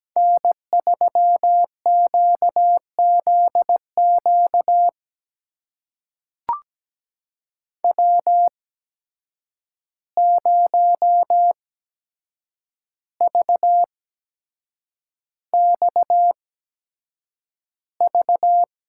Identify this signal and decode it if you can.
Morse code, heard as a keyed tone, about 17 wpm, spaced as for 7 wpm, 700 Hz. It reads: N3QZQ W0VXV
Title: USA Call Signs - 5 Characters - 8x Character Spacing - 17wpm